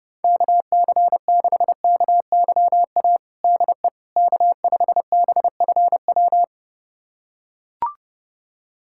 Morse code, heard as a keyed tone, about 30 wpm, 700 Hz. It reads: KC6KYA DE K5BFW
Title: USA Callsign DE Callsign 30wpm